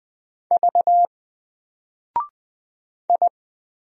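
Morse code, heard as a keyed tone, about 20 wpm, 700 Hz. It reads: V I
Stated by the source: Single letter and number Morse drill at 20 wpm, from N4LCD.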